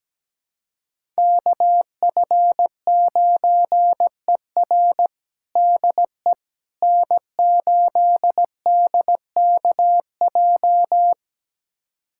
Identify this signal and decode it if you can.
Morse code, heard as a keyed tone, about 17 wpm, 700 Hz. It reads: KF9ER DE N8DKJ
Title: USA Callsign DE Callsign 17wpm